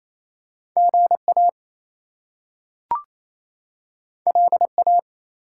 Morse code, heard as a keyed tone, about 28 wpm, 700 Hz. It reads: GA LA
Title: US States and Canadian Provinces 28wpm